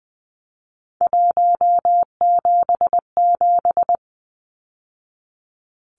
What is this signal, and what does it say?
Morse code, heard as a keyed tone, about 20 wpm, 700 Hz. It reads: 177
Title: Numbers - 3-Digit 20wpm